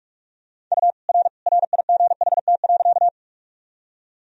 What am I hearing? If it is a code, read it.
Morse code, heard as a keyed tone, about 45 wpm, 700 Hz. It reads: U R RIGHT.
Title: Sentences from Top 200 Words and Abbreviations 45wpm